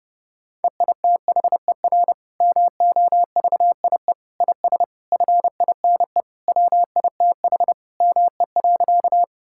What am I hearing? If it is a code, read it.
Morse code, heard as a keyed tone, about 30 wpm, 700 Hz. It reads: EITHER MOVIE IS FINE WITH ME.
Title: Sentences from Top 1000 Words - Review Part 5 - 30wpm